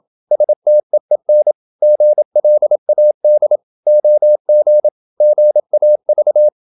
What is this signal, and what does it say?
Morse code, heard as a keyed tone, about 27 wpm, 600 Hz. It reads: STEEN GLAD OG GAV
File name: Ferie QRQ 27wpm